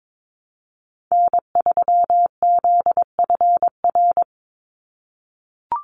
Morse code, heard as a keyed tone, about 22 wpm, 700 Hz. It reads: N3ZFR
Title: USA Call Signs 22wpm